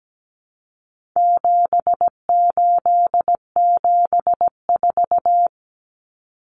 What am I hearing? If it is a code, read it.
Morse code, heard as a keyed tone, about 17 wpm, 700 Hz. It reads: 7874
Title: Common Cut Numbers - 4-Digit 17wpm